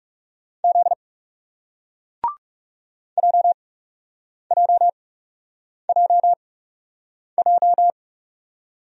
Morse code, heard as a keyed tone, about 40 wpm, spaced as for 8 wpm, 700 Hz. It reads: Z JJJJ